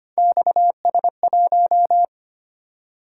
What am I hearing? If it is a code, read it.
Morse code, heard as a keyed tone, about 29 wpm, 700 Hz. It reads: XS1
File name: Speed-Racing - Three Letter-Number Combinations - 25wpm